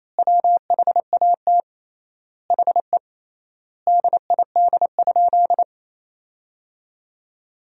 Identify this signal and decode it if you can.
Morse code, heard as a keyed tone, about 27 wpm, 700 Hz. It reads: WHAT HE DID?
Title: 3x Word-Spacing - Sentences from Top 100 Words - 28wpm